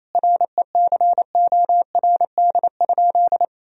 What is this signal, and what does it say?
Morse code, heard as a keyed tone, about 27 wpm, 700 Hz. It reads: RECORD?
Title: Sentences from Top 1000 Words - Review Part 3 - 28wpm